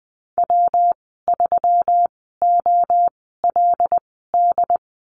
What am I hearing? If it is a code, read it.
Morse code, heard as a keyed tone, about 20 wpm, spaced as for 14 wpm, 700 Hz. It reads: W3OLD
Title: 140 Lesson 23g - Letter f - Call Signs 20wpm